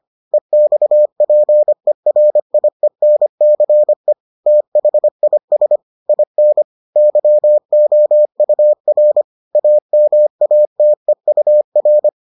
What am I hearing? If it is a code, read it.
Morse code, heard as a keyed tone, about 25 wpm, 600 Hz. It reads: EXPERIENCE THIS IN YOUR AMATEUR